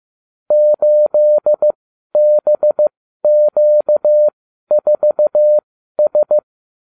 Morse code, heard as a keyed tone, about 15 wpm, spaced as for 12 wpm, 600 Hz. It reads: 8BQ4S